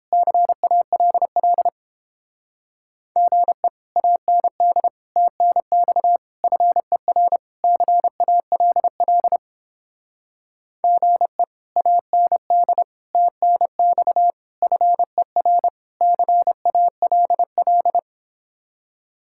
Morse code, heard as a keyed tone, about 29 wpm, 700 Hz. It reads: CALL GE AND TNX FER CALL GE AND TNX FER CALL